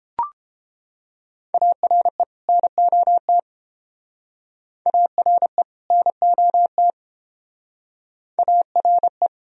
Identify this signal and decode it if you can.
Morse code, heard as a keyed tone, about 30 wpm, 700 Hz. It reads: ARE NOT ARE NOT ARE